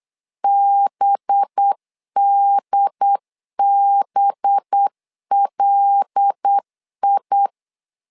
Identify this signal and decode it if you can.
Morse code, heard as a keyed tone, about 8 wpm, 800 Hz. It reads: BDBLI
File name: LEZIONI / 5 LBVC / 5 LB 8wpm 2816